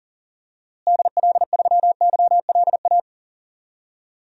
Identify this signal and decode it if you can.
Morse code, heard as a keyed tone, about 40 wpm, 700 Hz. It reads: DL3YLA
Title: DXCC-100 Call Signs - 6 Characters - Repeated 2x - 40wpm